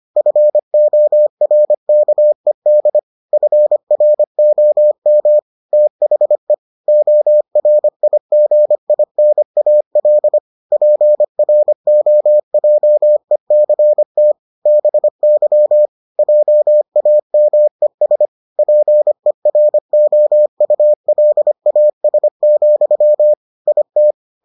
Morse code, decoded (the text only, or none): FORKED FROM THE ORIGINAL PROJECT BY JAMES PEROULAS, IT